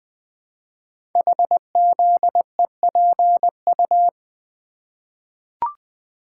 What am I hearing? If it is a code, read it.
Morse code, heard as a keyed tone, about 20 wpm, 700 Hz. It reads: HZEPU
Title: Code Groups - 5 Letters - 20wpm